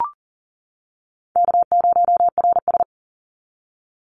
Morse code, heard as a keyed tone, about 41 wpm, 700 Hz. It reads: K0RS
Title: NA SOTA Chaser Callsigns 40wpm